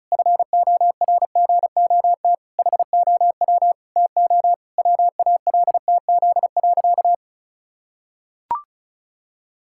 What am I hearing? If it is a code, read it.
Morse code, heard as a keyed tone, about 34 wpm, 700 Hz. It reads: FORGOT HOW TO WALTZ.
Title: Pangram Challenge v2 - 35wpm